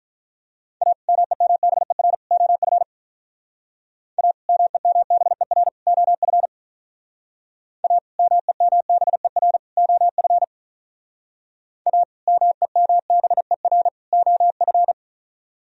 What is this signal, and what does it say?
Morse code, heard as a keyed tone, about 46 wpm, 700 Hz. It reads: A MEMBER OF A MEMBER OF A MEMBER OF A MEMBER OF